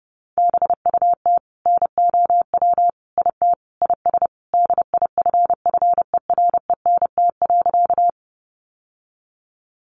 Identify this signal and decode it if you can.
Morse code, heard as a keyed tone, about 30 wpm, 700 Hz. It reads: BUT NOW IT IS DIFFERENT.
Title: Sentences from Top 600 Words 30wpm